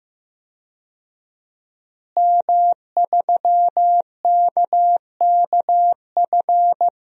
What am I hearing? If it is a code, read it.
Morse code, heard as a keyed tone, about 15 wpm, 700 Hz. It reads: M3KKF
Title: DXCC-100 Call Signs - 5 Characters - Repeated 3x - 15wpm